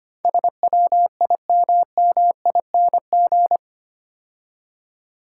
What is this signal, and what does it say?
Morse code, heard as a keyed tone, about 29 wpm, 690 Hz. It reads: SWIMMING
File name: practice / 25wpm / Difficult Words - Speed-Racing with Hints - 25wpm